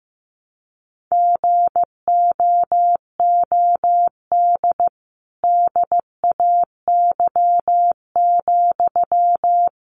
Morse code, heard as a keyed tone, about 15 wpm, 700 Hz. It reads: GOOD DAY,
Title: Sentences from Top 200 Words 15wpm